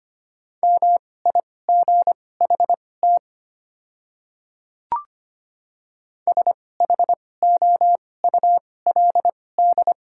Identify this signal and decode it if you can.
Morse code, heard as a keyed tone, about 25 wpm, spaced as for 18 wpm, 700 Hz. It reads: MIGHT SHOULD
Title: 2x Character Spacing - Top 300 Words - 25wpm